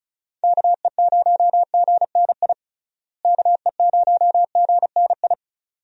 Morse code, heard as a keyed tone, about 36 wpm, 700 Hz. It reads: KE0GNI KE0GNI
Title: USA Call Signs - 6 Characters - Repeated 2x - 35wpm